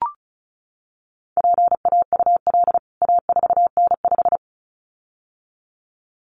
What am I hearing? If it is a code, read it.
Morse code, heard as a keyed tone, about 34 wpm, 700 Hz. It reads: PAUL A4N5
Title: CWT Exchange 35wpm